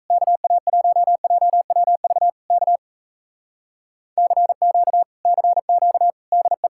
Morse code, heard as a keyed tone, about 37 wpm, 700 Hz. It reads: KA1JWV K CQ CQ DE